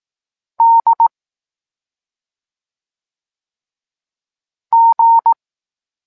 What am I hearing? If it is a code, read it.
Morse code, heard as a keyed tone, about 18 wpm, 920 Hz. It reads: D G
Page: Tekst 23 – Doskonalenie odbioru poznanych liter